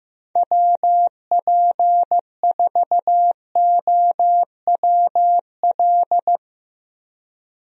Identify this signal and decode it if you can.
Morse code, heard as a keyed tone, about 15 wpm, 700 Hz. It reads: WP4OWL
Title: USA Call Signs - 6 Characters - Repeated 3x - 15wpm